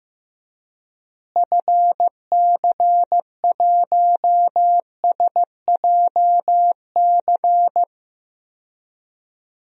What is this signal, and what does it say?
Morse code, heard as a keyed tone, about 15 wpm, 700 Hz. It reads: FC1SJC
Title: DXCC-100 Call Signs - 6 Characters - Repeated 3x - 15wpm